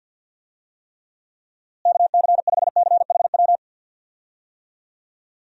Morse code, heard as a keyed tone, about 50 wpm, 700 Hz. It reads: KC5CHW
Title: USA Call Signs - 6 Characters - Repeated 3x - 50wpm